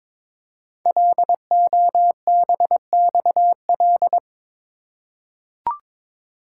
Morse code, heard as a keyed tone, about 23 wpm, 700 Hz. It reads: LOBXL
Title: Code Groups - 5 Letters - 22wpm